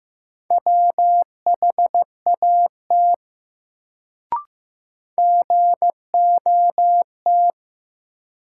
Morse code, heard as a keyed tone, about 16 wpm, 700 Hz. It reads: WHAT GOT
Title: Sets of 2 Words - New Word Drill - 15wpm